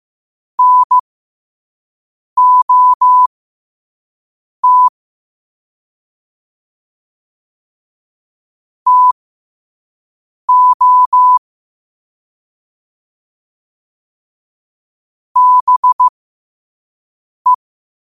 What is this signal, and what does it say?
Morse code, heard as a keyed tone, about 15 wpm, spaced as for 5 wpm, 1000 Hz. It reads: NOT TO BE
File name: qotdfiles / 5wpm